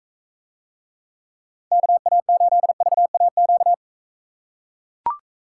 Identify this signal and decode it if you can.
Morse code, heard as a keyed tone, about 41 wpm, 700 Hz. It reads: KA8VAQ